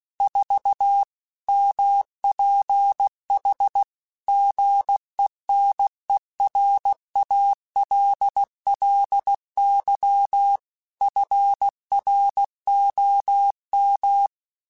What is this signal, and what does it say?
Morse code, heard as a keyed tone, about 16 wpm, 800 Hz. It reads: 4 MPH GENERALLY FROM
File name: Weather Sets / Weather Sets-mp3s / Weather 4 14 @15